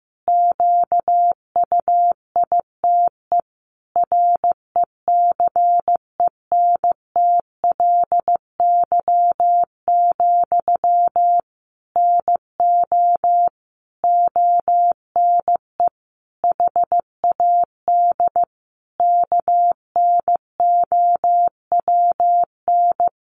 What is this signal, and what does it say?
Morse code, heard as a keyed tone, about 15 wpm, 700 Hz. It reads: QUITE RECENTLY, NO ONE HAD KNOWN
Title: Sentences from Top 1200 Words 15wpm